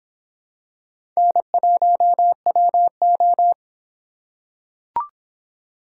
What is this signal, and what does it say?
Morse code, heard as a keyed tone, about 23 wpm, 700 Hz. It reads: N1WO